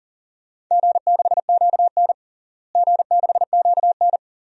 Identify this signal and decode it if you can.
Morse code, heard as a keyed tone, about 37 wpm, 700 Hz. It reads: G6QN G6QN
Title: DXCC-100 Call Signs - 4 Characters - Repeated 3x - 40wpm